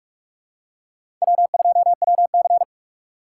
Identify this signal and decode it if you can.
Morse code, heard as a keyed tone, about 41 wpm, 700 Hz. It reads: W2WC
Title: USA Call Signs - 6x Character Spacing - 45wpm